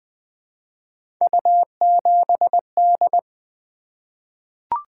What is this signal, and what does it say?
Morse code, heard as a keyed tone, about 20 wpm, 700 Hz. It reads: U7D
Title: Three Letter-Number Combinations - 20wpm